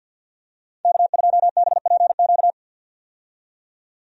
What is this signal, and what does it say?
Morse code, heard as a keyed tone, about 49 wpm, 700 Hz. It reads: K2BPQ
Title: USA Call Signs - 5 Characters - 50wpm